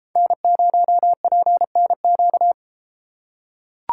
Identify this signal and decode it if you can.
Morse code, heard as a keyed tone, about 31 wpm, 700 Hz. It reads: N0PNQ